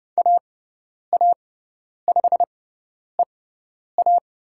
Morse code, heard as a keyed tone, about 29 wpm, spaced as for 10 wpm, 700 Hz. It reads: AA5EA
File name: Lesson 17g - Letter u - Call Signs 30wpm